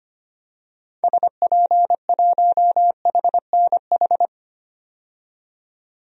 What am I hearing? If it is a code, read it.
Morse code, heard as a keyed tone, about 25 wpm, 700 Hz. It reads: SP1HNH
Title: DXCC-100 Call Signs - 6 Characters - 25wpm